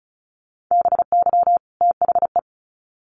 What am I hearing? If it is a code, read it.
Morse code, heard as a keyed tone, about 34 wpm, 700 Hz. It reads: BY THE
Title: Sets of 2 Words 35wpm